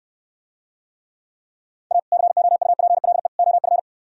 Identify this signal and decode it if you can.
Morse code, heard as a keyed tone, about 68 wpm, 700 Hz. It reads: A COUPLE OF